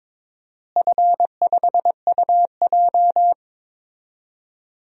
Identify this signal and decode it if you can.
Morse code, heard as a keyed tone, about 22 wpm, 700 Hz. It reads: F5UJ
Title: DXCC-100 Call Signs - 4 Characters - Repeated 2x - 22wpm